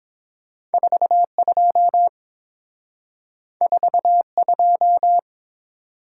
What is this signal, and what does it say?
Morse code, heard as a keyed tone, about 30 wpm, 700 Hz. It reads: 42 42